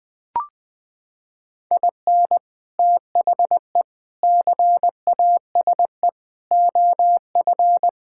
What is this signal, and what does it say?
Morse code, heard as a keyed tone, about 21 wpm, 700 Hz. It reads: IN THE CASE OF
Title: Sentences with 3-Word Phrases - 20wpm